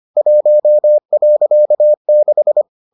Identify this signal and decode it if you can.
Morse code, heard as a keyed tone, about 25 wpm, 600 Hz. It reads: 1.6